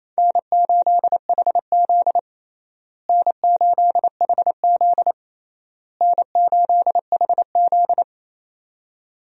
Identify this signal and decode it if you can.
Morse code, heard as a keyed tone, about 28 wpm, 700 Hz. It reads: N8HZ N8HZ N8HZ
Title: USA Call Signs - 4 Characters - Repeated 3x - 28wpm